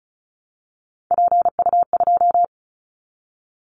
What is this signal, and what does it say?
Morse code, heard as a keyed tone, about 36 wpm, 700 Hz. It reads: PU2
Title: Callsign Prefixes for Top 50 DXCC - 35wpm